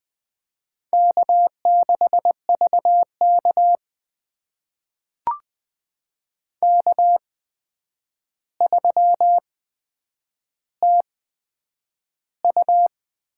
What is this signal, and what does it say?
Morse code, heard as a keyed tone, about 20 wpm, spaced as for 9 wpm, 700 Hz. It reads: K6VK K3TU
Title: USA Call Signs - 4 Characters - 8x Character Spacing - 20wpm